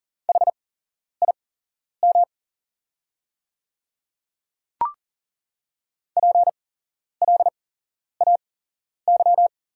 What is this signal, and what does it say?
Morse code, heard as a keyed tone, about 38 wpm, spaced as for 11 wpm, 700 Hz. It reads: HIM PLAY
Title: 8x Character Spacing - Top 200 Words - 40wpm